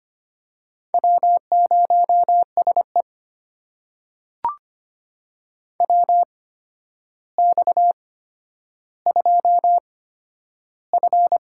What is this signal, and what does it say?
Morse code, heard as a keyed tone, about 25 wpm, spaced as for 11 wpm, 700 Hz. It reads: W0SE WX2F
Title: USA Call Signs - 4 Characters - 8x Character Spacing - 25wpm